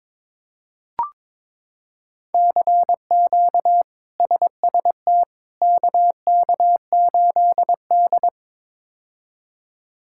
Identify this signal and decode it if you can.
Morse code, heard as a keyed tone, about 22 wpm, 700 Hz. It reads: CQ SST KK8D
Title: K1USN SST QSO Practice - 22wpm